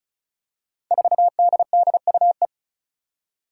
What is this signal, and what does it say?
Morse code, heard as a keyed tone, about 36 wpm, 700 Hz. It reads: 4DDUE
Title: Five Challenging Letter-Number Combinations 35wpm